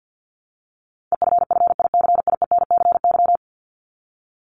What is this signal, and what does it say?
Morse code, heard as a keyed tone, about 48 wpm, 700 Hz. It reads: EFFICIENCY
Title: Top 1501-1750 Words 50wpm